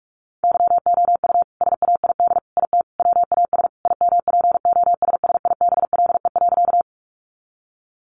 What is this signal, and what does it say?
Morse code, heard as a keyed tone, about 44 wpm, 700 Hz. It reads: YOU SAID IT WAS IMPOSSIBLE.
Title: Sentences from Top 1100 Words 45wpm